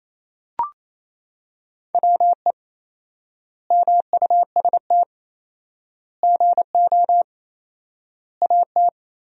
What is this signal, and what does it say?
Morse code, heard as a keyed tone, about 28 wpm, 700 Hz. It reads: WE MUST GO AT
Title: 4x Word-Spacing - Sentences from Top 300 Words - 28wpm